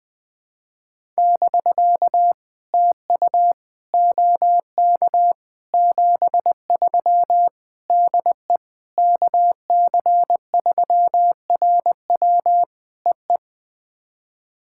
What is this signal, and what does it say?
Morse code, heard as a keyed tone, about 20 wpm, 700 Hz. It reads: <BK> TU OK 73 DE KC3RW EE
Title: POTA QSO Practice - 20wpm